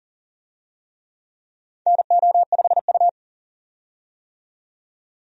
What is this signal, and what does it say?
Morse code, heard as a keyed tone, about 40 wpm, 690 Hz. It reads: NO5U
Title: USA Call Signs - 4 Characters - 40wpm